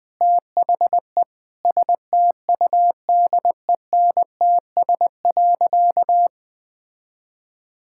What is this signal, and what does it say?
Morse code, heard as a keyed tone, about 20 wpm, 700 Hz. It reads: THE STUDENTS.
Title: Sentences with 2-Word Phrases - 20wpm